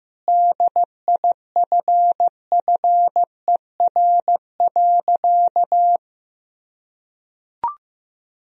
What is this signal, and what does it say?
Morse code, heard as a keyed tone, about 15 wpm, 700 Hz. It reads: DIFFER.
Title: Sentences from Top 1000 Words - Review Part 1 - 15wpm